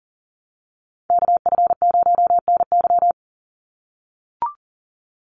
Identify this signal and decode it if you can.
Morse code, heard as a keyed tone, about 40 wpm, 700 Hz. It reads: KF0NY
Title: USA Call Signs 40wpm